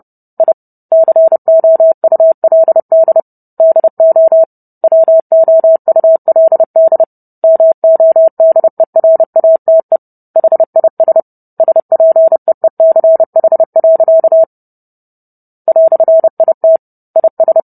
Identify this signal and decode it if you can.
Morse code, heard as a keyed tone, about 30 wpm, 650 Hz. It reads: I COULD DO WOULD MODERATE HIS SPEECH. "IT IS